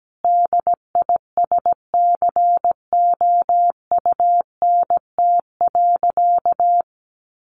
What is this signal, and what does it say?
Morse code, heard as a keyed tone, about 17 wpm, 700 Hz. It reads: DISCOUNT.
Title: Sentences from Top 1200 Words 17wpm